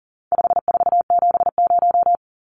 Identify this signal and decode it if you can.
Morse code, heard as a keyed tone, about 40 wpm, 700 Hz. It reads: 5470
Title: Numbers - 4-Digit 40wpm